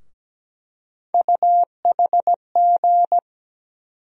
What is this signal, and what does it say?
Morse code, heard as a keyed tone, about 17 wpm, 700 Hz. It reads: UHG
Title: USA Call Sign Postfixes - 3 Letters 17wpm